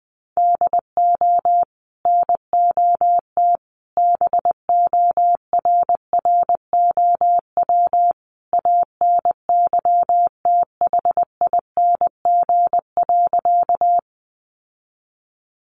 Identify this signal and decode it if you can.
Morse code, heard as a keyed tone, about 20 wpm, 700 Hz. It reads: DO NOT BORROW ANYTHING.
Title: Sentences from Top 1500 Words 20wpm